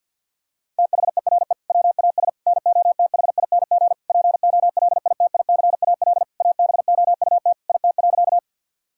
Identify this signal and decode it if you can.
Morse code, heard as a keyed tone, about 50 wpm, 700 Hz. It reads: THERE WAS NOTHING POLITICAL ABOUT IT.
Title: Sentences from Top 1000 Words - Review Part 5 - 50wpm